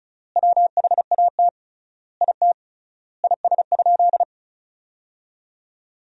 Morse code, heard as a keyed tone, about 36 wpm, 700 Hz. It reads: WHAT IT IS?
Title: 3x Word-Spacing - Sentences from Top 100 Words - 35wpm